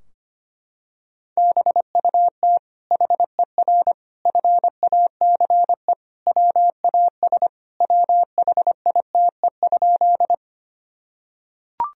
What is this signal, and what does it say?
Morse code, heard as a keyed tone, about 25 wpm, 700 Hz. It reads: BUT HER FACE WAS WHITE?
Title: Sentences from Top 1000 Words - Review Part 2 - 25wpm